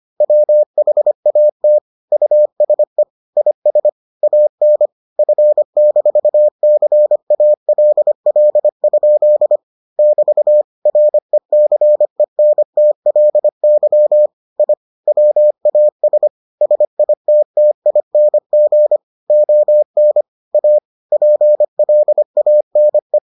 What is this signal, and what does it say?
Morse code, heard as a keyed tone, about 25 wpm, 600 Hz. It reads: WHAT USE IS AN F-CALL? <BT> RECENTLY I WAS SITTING ON A PLANE